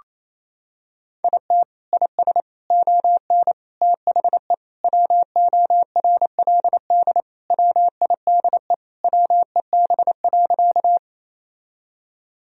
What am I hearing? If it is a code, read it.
Morse code, heard as a keyed tone, about 28 wpm, 700 Hz. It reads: IT IS ON THE WORLD WIDE WEB.
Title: Sentences from Top 1000 Words - Review Part 4 - 28wpm